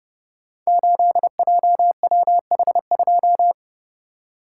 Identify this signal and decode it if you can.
Morse code, heard as a keyed tone, about 30 wpm, 700 Hz. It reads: 8JWH2
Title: Code Groups - 5 Letter:Numbers - 30wpm